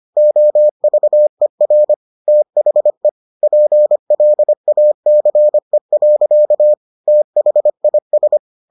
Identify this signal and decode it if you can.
Morse code, heard as a keyed tone, about 25 wpm, 600 Hz. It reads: OVER THE PLACE. THIS